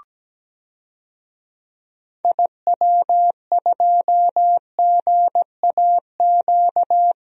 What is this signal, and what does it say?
Morse code, heard as a keyed tone, about 17 wpm, 700 Hz. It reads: IW2GAQ
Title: DXCC-100 Call Signs - 6 Characters - Repeated 3x - 17wpm